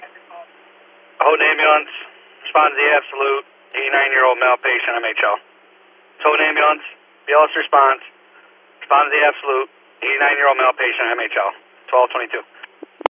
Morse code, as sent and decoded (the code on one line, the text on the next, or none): none